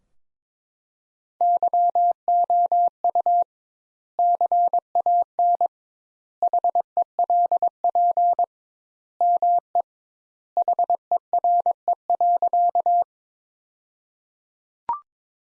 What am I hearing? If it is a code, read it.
Morse code, heard as a keyed tone, about 22 wpm, 700 Hz. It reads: YOU CAN HELP ME HERE.
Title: Sentences Top 200 Words - 22wpm